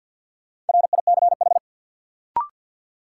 Morse code, heard as a keyed tone, about 52 wpm, 700 Hz. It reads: RICH
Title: Sentences from Top 1100 Words - New Word Drill - 50wpm